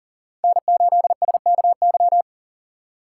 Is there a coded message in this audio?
Yes: Morse code from 0.4 to 2.2 s, heard as a keyed tone, about 52 wpm, 700 Hz. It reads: N8SKY